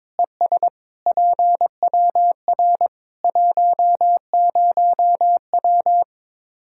Morse code, heard as a keyed tone, about 22 wpm, 700 Hz. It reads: ES PWR 10W
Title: Practice QSO - Exchange 2 - Info - 22wpm